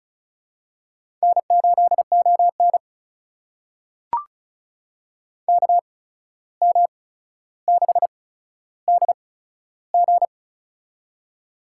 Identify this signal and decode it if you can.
Morse code, heard as a keyed tone, about 34 wpm, spaced as for 14 wpm, 700 Hz. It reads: N8ON KM6DG
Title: USA Call Signs - 8x Character Spacing - 35wpm